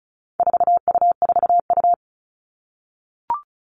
Morse code, heard as a keyed tone, about 34 wpm, 700 Hz. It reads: 4U4U
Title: All Cut Numbers - 4-Digit 35wpm